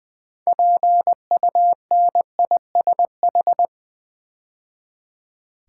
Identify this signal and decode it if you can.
Morse code, heard as a keyed tone, about 20 wpm, 700 Hz. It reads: PUNISH
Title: 6 Letter Words - First 3 Letters Given - 20wpm